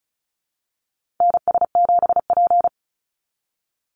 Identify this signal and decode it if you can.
Morse code, heard as a keyed tone, about 34 wpm, 700 Hz. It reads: NS7P
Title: NA SOTA Chaser Callsigns 35wpm